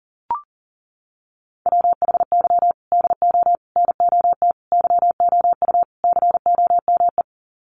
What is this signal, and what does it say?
Morse code, heard as a keyed tone, about 40 wpm, 700 Hz. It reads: WHY DO NOT YOU COME
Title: Sets of 5 Words Encore2 40wpm